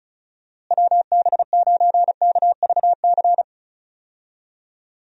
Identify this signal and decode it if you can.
Morse code, heard as a keyed tone, about 34 wpm, 700 Hz. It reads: WB9KVC
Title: USA Call Signs - 6 Characters - Repeated 3x - 35wpm